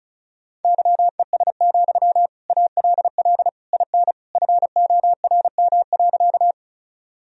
Morse code, heard as a keyed tone, about 36 wpm, 700 Hz. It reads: YES, ALL IN FORM.
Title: Sentences from Top 1000 Words - Review Part 1 - 35wpm